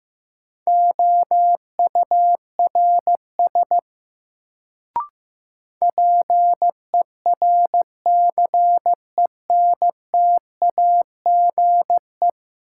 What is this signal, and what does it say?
Morse code, heard as a keyed tone, about 15 wpm, 700 Hz. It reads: OURS PERCENTAGE
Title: Sentences from Top 1300 Words - New Word Drill Alt - 15wpm